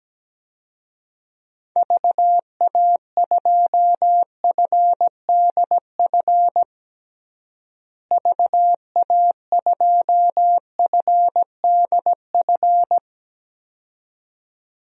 Morse code, heard as a keyed tone, about 17 wpm, 700 Hz. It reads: VA2FDF VA2FDF
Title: DXCC-100 Call Signs - 6 Characters - Repeated 2x - 17wpm